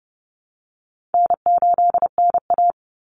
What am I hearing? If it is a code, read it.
Morse code, heard as a keyed tone, about 30 wpm, 700 Hz. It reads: N8NA
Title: Easy Call Signs - 30wpm